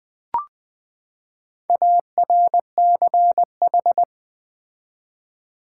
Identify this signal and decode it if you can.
Morse code, heard as a keyed tone, about 20 wpm, 700 Hz. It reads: ARCH
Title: Build A Word - 4 Letters At A Time - 12-Letter Words - 20wpm